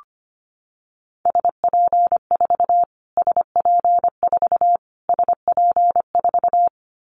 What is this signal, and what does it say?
Morse code, heard as a keyed tone, about 25 wpm, 700 Hz. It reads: SP4 SP4 SP4
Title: Callsign Prefixes for Top 25 DXCC - Repeated 3x - 25wpm